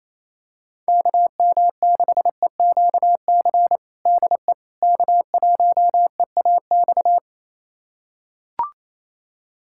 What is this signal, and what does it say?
Morse code, heard as a keyed tone, about 28 wpm, 700 Hz. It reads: KM6EQC DE K1EAX
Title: USA Callsign DE Callsign 28wpm